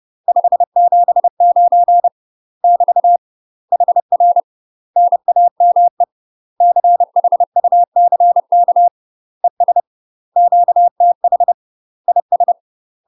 Morse code, heard as a keyed tone, about 30 wpm, 700 Hz. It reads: 579 <BT> HR NAME CHUCK ES QTH IS
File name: UK MASTER QSOs TRAINING / 30WPM / UK MASTER QSOs 0030